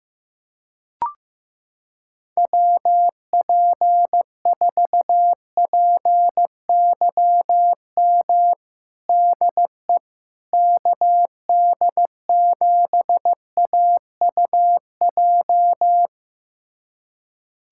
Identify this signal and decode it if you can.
Morse code, heard as a keyed tone, about 15 wpm, 700 Hz. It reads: WP4PYM DE KD7AUJ
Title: USA Callsign DE Callsign - 6 Characters - 15wpm